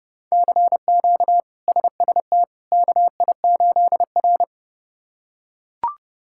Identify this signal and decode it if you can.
Morse code, heard as a keyed tone, about 30 wpm, 700 Hz. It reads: CQ SST KI8R
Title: K1USN SST QSO Practice - 30wpm